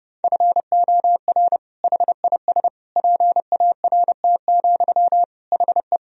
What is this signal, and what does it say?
Morse code, heard as a keyed tone, about 30 wpm, 700 Hz. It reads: FOR HIS PART, HE
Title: Sentences from Top 1000 Words - Review Part 2 - 30wpm